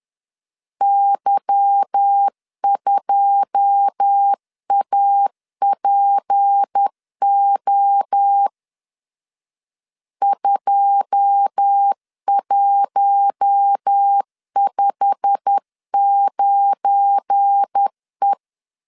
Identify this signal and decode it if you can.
Morse code, heard as a keyed tone, about 10 wpm, 800 Hz. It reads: Y2APO 2159E
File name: LEZIONI / 9 2 10wpm 2736